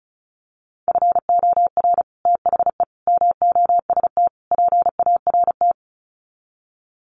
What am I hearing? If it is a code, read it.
Morse code, heard as a keyed tone, about 34 wpm, 700 Hz. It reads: FOR THE MOST PART
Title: Sets of 4 Words 35wpm